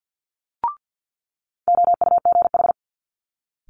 Morse code, heard as a keyed tone, about 50 wpm, 700 Hz. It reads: OUGH
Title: Multi-letter Phonemes Quick Fire 50wpm